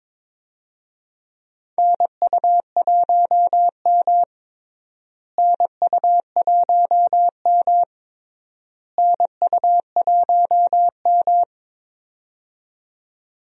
Morse code, heard as a keyed tone, about 22 wpm, 700 Hz. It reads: NU1M NU1M NU1M
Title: USA Call Signs - 4 Characters - Repeated 3x - 22wpm